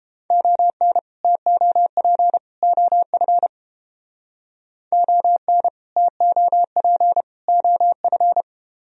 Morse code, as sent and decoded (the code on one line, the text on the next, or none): --- -. / - --- .--. / --- ..-. / --- -. / - --- .--. / --- ..-.
ON TOP OF ON TOP OF